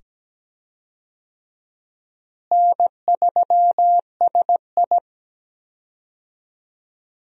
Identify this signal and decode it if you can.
Morse code, heard as a keyed tone, about 17 wpm, 700 Hz. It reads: N3SI
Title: USA Call Signs - 4 Characters - Repeated 2x - 17wpm